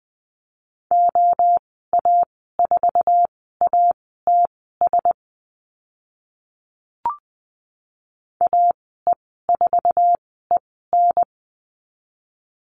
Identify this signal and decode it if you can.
Morse code, heard as a keyed tone, about 20 wpm, spaced as for 14 wpm, 700 Hz. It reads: OA4ATS AE4EN
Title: 069 Lesson 12g - Letter d - Call Signs 20wpm